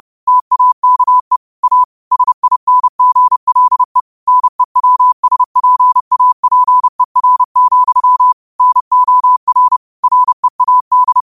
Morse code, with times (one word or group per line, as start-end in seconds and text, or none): 0.3-1.4 TAKE
1.6-1.8 A
2.1-4.0 SINGLE
4.3-8.3 NEWSPAPER,
8.6-9.8 NOR
10.0-11.2 READ